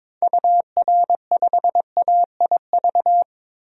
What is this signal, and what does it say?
Morse code, heard as a keyed tone, about 22 wpm, 700 Hz. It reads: UR5AIV
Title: DXCC-100 Call Signs - 6 Characters - 22wpm